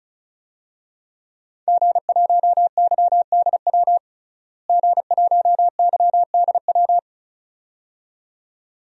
Morse code, heard as a keyed tone, about 36 wpm, 700 Hz. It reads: G1YDW G1YDW
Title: DXCC-100 Call Signs - 5 Characters - Repeated 2x - 35wpm